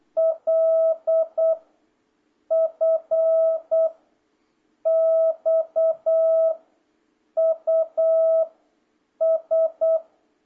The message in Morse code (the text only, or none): L F X U S